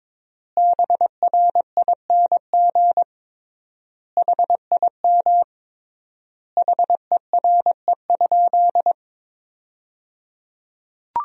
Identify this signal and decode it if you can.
Morse code, heard as a keyed tone, about 22 wpm, 700 Hz. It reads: BRING HIM HERE?
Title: 3x Word-Spacing - Sentences from Top 500 Words - 22wpm